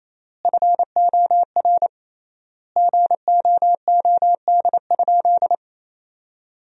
Morse code, heard as a keyed tone, about 27 wpm, 700 Hz. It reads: FOR GOOD?
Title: 3x Word-Spacing - Sentences from Top 200 Words - 28wpm